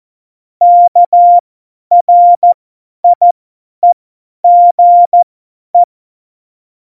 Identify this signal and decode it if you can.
Morse code, heard as a keyed tone, about 14 wpm, spaced as for 10 wpm, 700 Hz. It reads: KRIEGE
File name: Morsekurs / Witze / Witz 24 April 2018, 50 F70